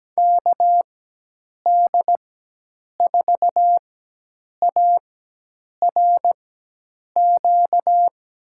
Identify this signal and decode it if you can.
Morse code, heard as a keyed tone, about 17 wpm, spaced as for 8 wpm, 700 Hz. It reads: KD4ARQ
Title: USA Call Signs - 6 Characters - 4x Character Spacing - 17wpm